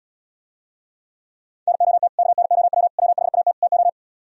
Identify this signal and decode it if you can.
Morse code, heard as a keyed tone, about 77 wpm, 700 Hz. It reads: NOT KNOW WHAT TO